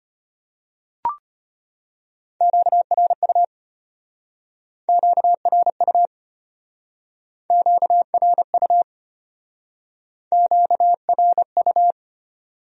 Morse code, heard as a keyed tone, about 32 wpm, 700 Hz. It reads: QRU QRU QRU QRU